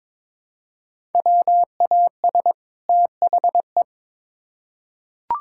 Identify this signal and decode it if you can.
Morse code, heard as a keyed tone, about 22 wpm, 700 Hz. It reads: WAS THE
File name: Build A Phrase - 2-Word Phrases - 22wpm